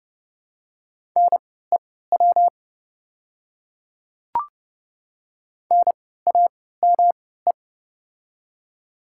Morse code, heard as a keyed tone, about 31 wpm, spaced as for 17 wpm, 690 Hz. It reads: NEW NAME
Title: 3x Character Spacing - Top 200 Words - 30wpm